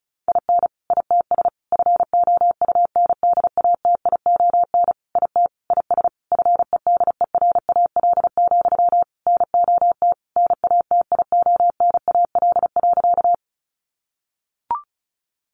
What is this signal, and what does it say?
Morse code, heard as a keyed tone, about 34 wpm, 700 Hz. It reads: IN ITS FOUNDATION IT IS FEDERAL, NOT NATIONAL.
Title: Sentences from Top 1200 Words 35wpm